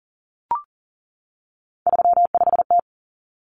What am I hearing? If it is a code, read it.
Morse code, heard as a keyed tone, about 39 wpm, 700 Hz. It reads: 35T
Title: Common Cut Numbers - 3-Digit 40wpm